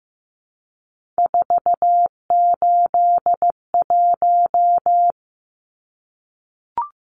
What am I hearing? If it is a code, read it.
Morse code, heard as a keyed tone, about 15 wpm, 700 Hz. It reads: 481
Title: Numbers - 3-Digit 15wpm